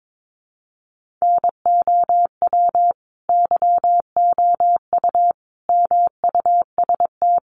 Morse code, heard as a keyed tone, about 22 wpm, 700 Hz. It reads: NOW YOU MUST
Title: Sentences from Top 1500 Words 22wpm